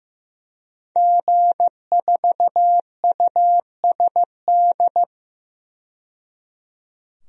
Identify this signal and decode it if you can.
Morse code, heard as a keyed tone, about 15 wpm, 700 Hz. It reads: G4USD
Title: DXCC-100 Call Signs - 5 Characters - 15wpm